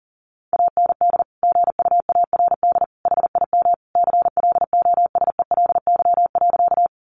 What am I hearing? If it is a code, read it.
Morse code, heard as a keyed tone, about 40 wpm, 700 Hz. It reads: AND GUARD HIM CLOSELY.
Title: Sentences from Top 1400 Words 40wpm